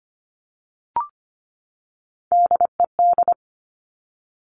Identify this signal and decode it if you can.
Morse code, heard as a keyed tone, about 25 wpm, 700 Hz. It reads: DED
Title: Top Three Letter Combinations in Words 25wpm